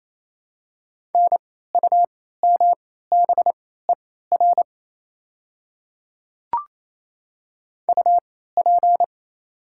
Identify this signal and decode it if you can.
Morse code, heard as a keyed tone, about 28 wpm, spaced as for 16 wpm, 700 Hz. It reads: NUMBER UP
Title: 3x Character Spacing - Top 100 Words - 28wpm